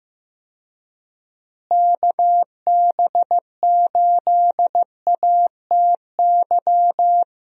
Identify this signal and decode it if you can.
Morse code, heard as a keyed tone, about 15 wpm, 700 Hz. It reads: KB8ATY
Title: USA Call Signs - 6 Characters - Repeated 2x - 15wpm